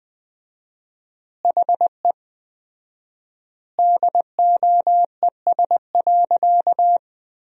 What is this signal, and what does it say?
Morse code, heard as a keyed tone, about 20 wpm, 700 Hz. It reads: HE DOES.
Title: 4x Word-Spacing - Sentences from Top 500 Words - 20wpm